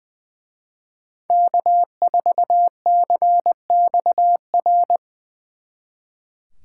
Morse code, heard as a keyed tone, about 24 wpm, 700 Hz. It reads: K4CXR
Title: Speed Racing - USA 1x3 Call Signs 20wpm